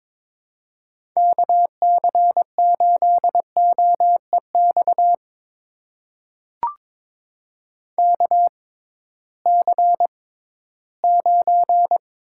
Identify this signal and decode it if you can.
Morse code, heard as a keyed tone, about 22 wpm, spaced as for 14 wpm, 700 Hz. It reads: KC8OEX KC9
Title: USA Call Signs - 6 Characters - 6x Character Spacing - 22wpm